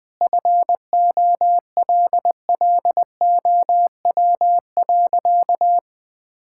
Morse code, heard as a keyed tone, about 20 wpm, 700 Hz. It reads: FOLLOW.
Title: Sentences from Top 1000 Words - Review Part 6 - 20wpm